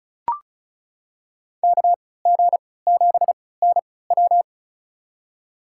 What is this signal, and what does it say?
Morse code, heard as a keyed tone, about 34 wpm, spaced as for 20 wpm, 690 Hz. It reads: KG7NW
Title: USA Call Signs - 3x Character Spacing - 35wpm